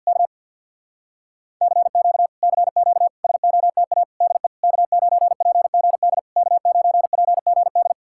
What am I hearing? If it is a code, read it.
Morse code, heard as a keyed tone, about 67 wpm, spaced as for 58 wpm, 700 Hz. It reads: K CQ CQ SOTA DE K9PGD K9PGD
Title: Speed-Racing Common QSO Phrases - 50wpm